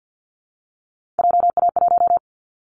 Morse code, heard as a keyed tone, about 50 wpm, 700 Hz. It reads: JA1
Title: Callsign Prefixes for Top 25 DXCC - Repeated 3x - 50wpm